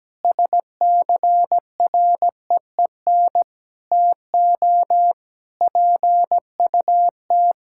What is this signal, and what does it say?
Morse code, heard as a keyed tone, about 17 wpm, 700 Hz. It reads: SCREEN TO PUT
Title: Sentences from Top 1000 Words - Review Part 6 - 17wpm